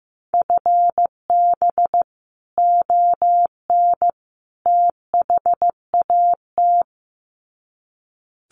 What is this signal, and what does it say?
Morse code, heard as a keyed tone, about 15 wpm, 700 Hz. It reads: FB ON THAT
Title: Common QSO Phrases 15wpm